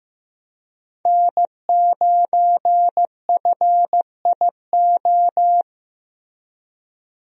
Morse code, heard as a keyed tone, about 15 wpm, 700 Hz. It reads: N9FIO
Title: USA Call Signs - 5 Characters - Repeated 3x - 15wpm